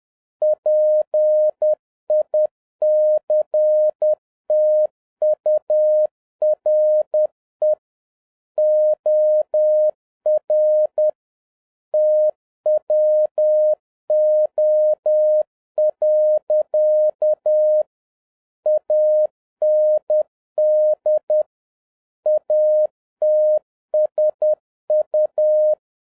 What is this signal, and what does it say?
Morse code, heard as a keyed tone, about 10 wpm, 610 Hz. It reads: PICTURE OR TWO. AND ATSU
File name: 2011-07-mp3 / n46 JOHN K3TN 10wpm